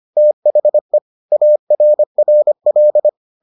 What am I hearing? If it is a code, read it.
Morse code, heard as a keyed tone, about 25 wpm, 600 Hz. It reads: THE ARRL